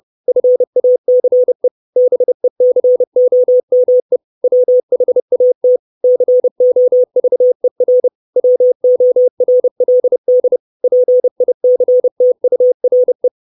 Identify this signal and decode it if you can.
Morse code, heard as a keyed tone, about 30 wpm, 500 Hz. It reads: FACE BECOME WHAT COVER WORLD PICTURE
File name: 632 Words 30WPM